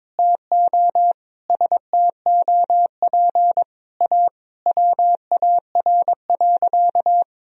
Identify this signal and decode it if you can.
Morse code, heard as a keyed tone, about 22 wpm, 700 Hz. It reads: TO STOP A WAR.
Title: Sentences from Top 1000 Words - Review Part 2 - 22wpm